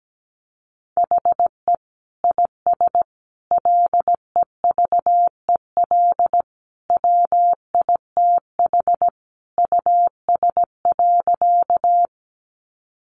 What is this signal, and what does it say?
Morse code, heard as a keyed tone, about 17 wpm, 700 Hz. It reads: HE IS LEVEL WITH US.
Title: Sentences from Top 700 Words 17wpm